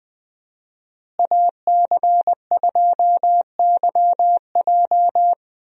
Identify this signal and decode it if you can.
Morse code, heard as a keyed tone, about 20 wpm, 700 Hz. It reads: AC2YJ
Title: USA Call Signs - 5 Characters - Repeated 2x - 20wpm